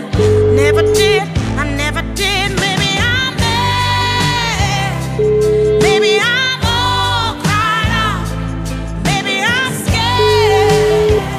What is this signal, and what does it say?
Ringback tone (Europe, 425 Hz), with noise.